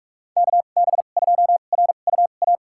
Morse code, heard as a keyed tone, about 45 wpm, spaced as for 32 wpm, 700 Hz. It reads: KB2RUA
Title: USA Call Signs - 2x Character Spacing - 45wpm